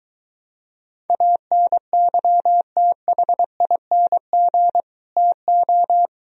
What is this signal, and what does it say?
Morse code, heard as a keyed tone, about 23 wpm, 700 Hz. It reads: ANYTHING TO